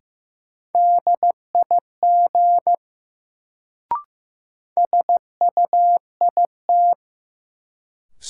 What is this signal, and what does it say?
Morse code, heard as a keyed tone, about 15 wpm, 700 Hz. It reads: DIG SUIT
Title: Sentences from Top 1600 Words - New Word Drill - 15wpm